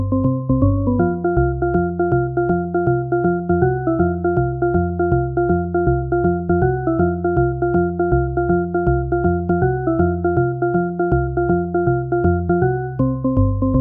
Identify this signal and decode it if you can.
There is no code.